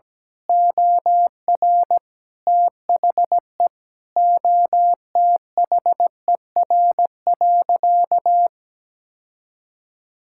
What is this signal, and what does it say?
Morse code, heard as a keyed tone, about 17 wpm, 700 Hz. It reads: OR THE OTHER.
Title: Sentences from Top 1000 Words - Review Part 6 - 17wpm